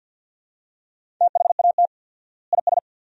Morse code, heard as a keyed tone, about 52 wpm, 700 Hz. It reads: THAT IS
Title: 4x Word-Spacing - Sentences from Top 200 Words - 50wpm